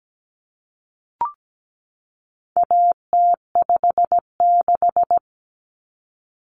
Morse code, heard as a keyed tone, about 17 wpm, 700 Hz. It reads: AT56
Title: Common Cut Numbers - 4-Digit 17wpm